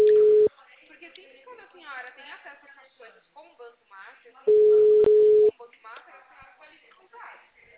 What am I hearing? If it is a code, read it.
Ringback tone (Europe, 425 Hz), with speech.
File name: out- -2007